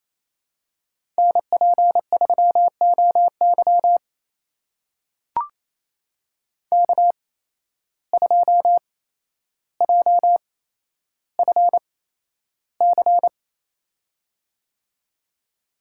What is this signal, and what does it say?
Morse code, heard as a keyed tone, about 27 wpm, spaced as for 12 wpm, 700 Hz. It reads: NP3OY K2JFC
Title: USA Call Signs - 5 Characters - 8x Character Spacing - 28wpm